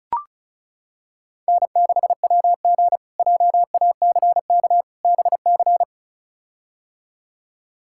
Morse code, heard as a keyed tone, about 36 wpm, 700 Hz. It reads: N6WG JACK BC
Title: K1USN SST QSO Practice - 35wpm